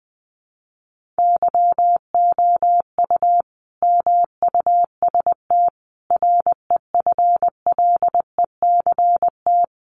Morse code, heard as a keyed tone, about 19 wpm, 700 Hz. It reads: YOU MUST REFLECT
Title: Sentences from Top 1500 Words 20wpm